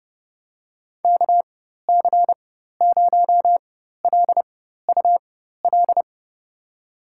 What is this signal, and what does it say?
Morse code, heard as a keyed tone, about 30 wpm, spaced as for 14 wpm, 700 Hz. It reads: KC0LUL
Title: USA Call Signs - 6 Characters - 4x Character Spacing - 30wpm